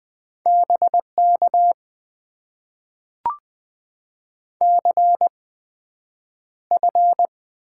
Morse code, heard as a keyed tone, about 20 wpm, spaced as for 9 wpm, 700 Hz. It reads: BK CF
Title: Top 90 QSO Elements - 8x Character Spacing - 20wpm